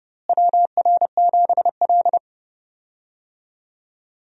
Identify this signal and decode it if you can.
Morse code, heard as a keyed tone, about 31 wpm, 700 Hz. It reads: WR7L
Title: USA Call Signs - 4 Characters - Repeated 3x - 30wpm